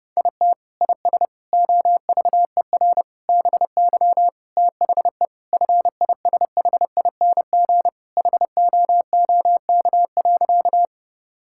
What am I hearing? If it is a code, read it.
Morse code, heard as a keyed tone, about 30 wpm, 700 Hz. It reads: IT IS OVER BY THE FISHING HOOK.
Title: Sentences from Top 1600 Words 30wpm